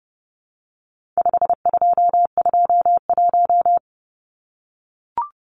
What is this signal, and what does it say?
Morse code, heard as a keyed tone, about 30 wpm, 700 Hz. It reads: 5221
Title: Numbers - 4-Digit 30wpm